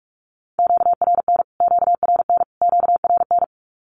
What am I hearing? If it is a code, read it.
Morse code, heard as a keyed tone, about 44 wpm, 700 Hz. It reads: QRN QRN QRN
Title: Q-Codes Repeated 3x 45wpm